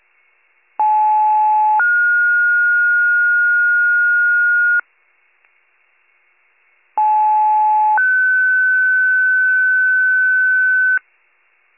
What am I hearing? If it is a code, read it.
Two-tone page, heard as a one-second tone, then a three-second tone, about 860 then 1430 Hz, about 860 then 1500 Hz, with noise. It is heard from 0.8 to 11.0 s.